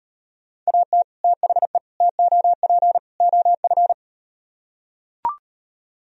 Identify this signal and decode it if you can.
Morse code, heard as a keyed tone, about 38 wpm, 700 Hz. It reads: AT THE TOP OF